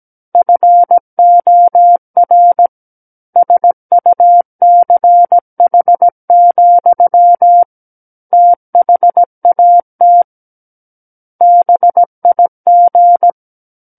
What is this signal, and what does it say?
Morse code, heard as a keyed tone, about 17 wpm, 700 Hz. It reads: FOR SUCH, THAT BIG
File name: Book One / 08-Chapter 8 - 17 WPM